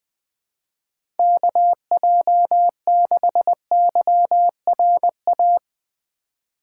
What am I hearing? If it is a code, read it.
Morse code, heard as a keyed tone, about 20 wpm, 700 Hz. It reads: KJ6YRA
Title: USA Call Signs - 6 Characters - 3x Character Spacing - 20wpm